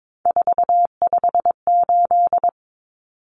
Morse code, heard as a keyed tone, about 22 wpm, 700 Hz. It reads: 458
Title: Numbers - 3-Digit 22wpm